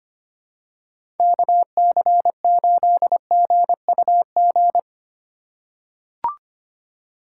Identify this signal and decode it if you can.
Morse code, heard as a keyed tone, about 25 wpm, 700 Hz. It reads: KC8GUG
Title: USA Call Signs - 6 Characters - 3x Character Spacing - 25wpm